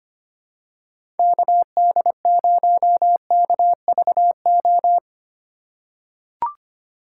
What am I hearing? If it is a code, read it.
Morse code, heard as a keyed tone, about 25 wpm, 700 Hz. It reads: KD0KVO
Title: USA Call Signs - 6 Characters - 4x Character Spacing - 25wpm